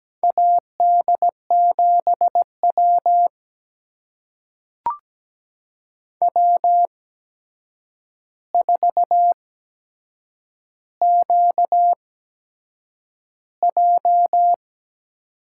Morse code, heard as a keyed tone, about 17 wpm, spaced as for 7 wpm, 700 Hz. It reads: AD7W W4QJ
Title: USA Call Signs - 4 Characters - 8x Character Spacing - 17wpm